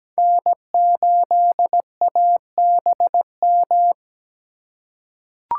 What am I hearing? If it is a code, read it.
Morse code, heard as a keyed tone, about 17 wpm, 700 Hz. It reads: N8ABM
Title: USA Call Signs - 5 Characters - 2x Character Spacing - 17wpm